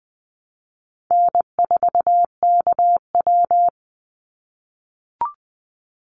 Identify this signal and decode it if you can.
Morse code, heard as a keyed tone, about 20 wpm, 700 Hz. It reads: N4KW
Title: Call Signs - 20wpm